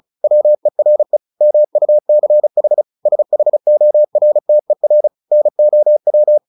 Morse code, heard as a keyed tone, about 34 wpm, 600 Hz. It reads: WERE MUCH SHORTER NOW
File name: the old man and the sea 2 35wpm